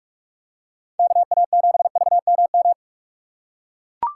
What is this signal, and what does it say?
Morse code, heard as a keyed tone, about 43 wpm, 700 Hz. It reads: KA7VMM
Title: USA Call Signs - 6 Characters - 2x Character Spacing - 45wpm